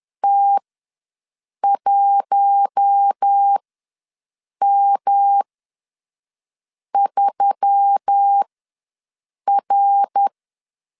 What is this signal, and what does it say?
Morse code, heard as a keyed tone, about 10 wpm, spaced as for 5 wpm, 800 Hz. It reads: T1M3R
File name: LEZIONI / 5 LBVC / 5 L ws 10wpm 3636